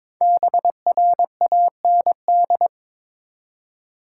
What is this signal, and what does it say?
Morse code, heard as a keyed tone, about 22 wpm, 700 Hz. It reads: BRAND
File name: 5 Letter Words - First 3 Letters Given - 22wpm